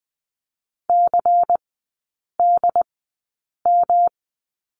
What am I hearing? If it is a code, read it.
Morse code, heard as a keyed tone, about 20 wpm, spaced as for 8 wpm, 700 Hz. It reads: CDM
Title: 121 Lesson 20e - Number 3 - Three Characters 20wpm